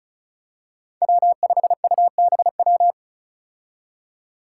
Morse code, heard as a keyed tone, about 36 wpm, 700 Hz. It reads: W5UBW
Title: Five Challenging Letter-Number Combinations 35wpm